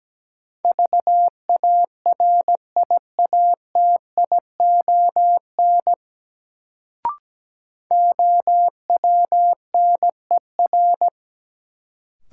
Morse code, heard as a keyed tone, about 17 wpm, 700 Hz. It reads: VARIATION OWNER
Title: Sentences from Top 1600 Words - New Word Drill Alt - 17wpm